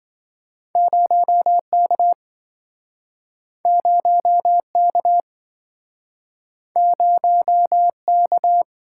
Morse code, heard as a keyed tone, about 27 wpm, 700 Hz. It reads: 0K 0K 0K